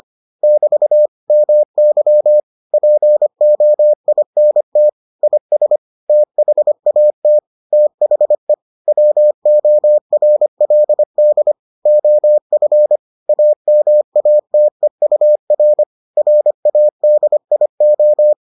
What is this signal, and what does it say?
Morse code, heard as a keyed tone, about 25 wpm, 600 Hz. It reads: <BT> MY POINT IS THAT THE WORLD OF AMATEUR RADIO